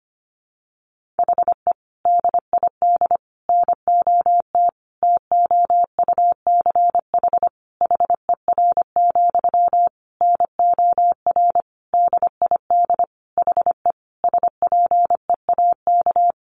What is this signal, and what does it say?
Morse code, heard as a keyed tone, about 25 wpm, 700 Hz. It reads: HE DID NOT TOUCH HER, NOR DID HE SPEAK
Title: Sentences from Top 1500 Words 25wpm